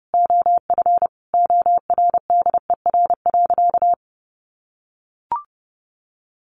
Morse code, heard as a keyed tone, about 30 wpm, 700 Hz. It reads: OF ORDER.
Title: Sentences from Top 1400 Words 30wpm